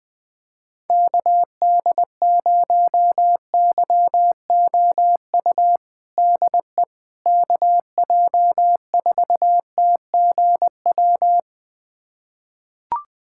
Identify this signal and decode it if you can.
Morse code, heard as a keyed tone, about 20 wpm, 700 Hz. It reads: KD0YOU DE KJ4TGW
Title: USA Callsign DE Callsign - 6 Characters - 20wpm